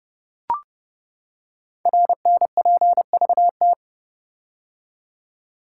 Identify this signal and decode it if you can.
Morse code, heard as a keyed tone, about 30 wpm, 690 Hz. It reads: RNPVT
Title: Code Groups - 5 Letters - 30wpm